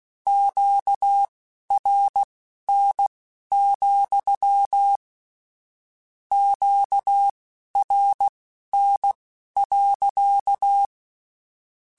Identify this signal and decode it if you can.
Morse code, heard as a keyed tone, about 16 wpm, spaced as for 12 wpm, 800 Hz. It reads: QRN, QRN.
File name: Q Calls / Q Calls-mp3s / Q Calls sppc @15